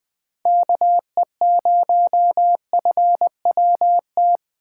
Morse code, heard as a keyed tone, about 20 wpm, 700 Hz. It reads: KE0FWT